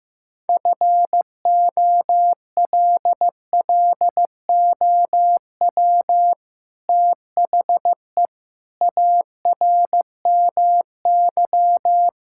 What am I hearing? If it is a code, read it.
Morse code, heard as a keyed tone, about 15 wpm, 700 Hz. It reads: FOLLOW THE ARMY
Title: Sentences from Top 1000 Words - Review Part 5 - 15wpm